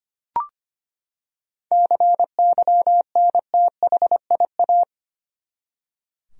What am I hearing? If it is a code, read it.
Morse code, heard as a keyed tone, about 25 wpm, 700 Hz. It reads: CYNTHIA
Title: USA Common Names - 25wpm